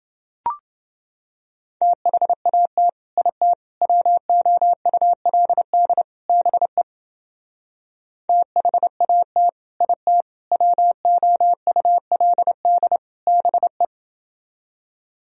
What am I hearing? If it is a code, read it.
Morse code, heard as a keyed tone, about 28 wpm, 700 Hz. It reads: THAT IT WOULD BE THAT IT WOULD BE